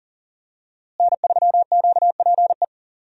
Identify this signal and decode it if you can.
Morse code, heard as a keyed tone, about 40 wpm, 700 Hz. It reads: N3QPE
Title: USA Call Signs - 5 Characters - Repeated 2x - 40wpm